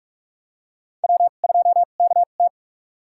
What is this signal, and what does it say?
Morse code, heard as a keyed tone, about 45 wpm, spaced as for 32 wpm, 700 Hz. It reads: W2KT
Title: USA Call Signs - 4 Characters - 2x Character Spacing - 45wpm